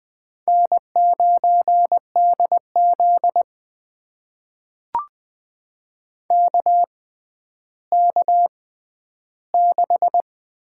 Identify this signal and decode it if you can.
Morse code, heard as a keyed tone, about 20 wpm, spaced as for 11 wpm, 700 Hz. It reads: N9DZ KK6
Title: USA Call Signs - 6x Character Spacing - 20wpm